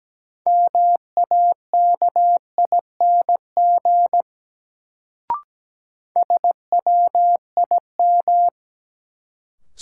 Morse code, heard as a keyed tone, about 17 wpm, 700 Hz. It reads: MAKING SWIM
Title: Sentences from Top 1000 Words - Review Part 6 - New Word Drill - 17wpm